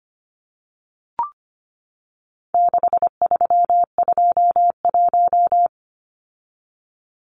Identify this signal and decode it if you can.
Morse code, heard as a keyed tone, about 25 wpm, 700 Hz. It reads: 6321
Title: Numbers - 4-Digit 25wpm